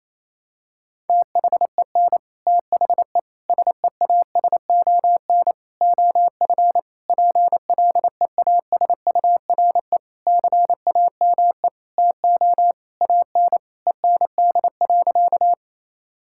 Morse code, heard as a keyed tone, about 28 wpm, 700 Hz. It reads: THEN THE SEASON OF PLEASURE CAME TO AN END.
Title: Sentences from Top 1600 Words 28wpm